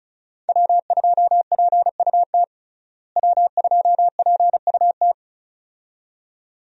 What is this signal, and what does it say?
Morse code, heard as a keyed tone, about 34 wpm, 700 Hz. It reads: W2PUT W2PUT
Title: USA Call Signs - 5 Characters - Repeated 2x - 35wpm